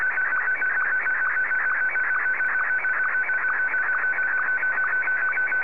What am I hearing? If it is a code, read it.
SSTV (slow-scan television), Martin 1.
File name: sstv / M1-VK2OY